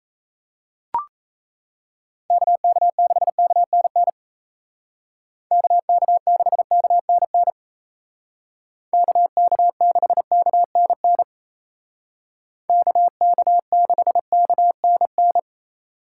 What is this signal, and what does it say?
Morse code, heard as a keyed tone, about 36 wpm, 700 Hz. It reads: KK6KNN KK6KNN KK6KNN KK6KNN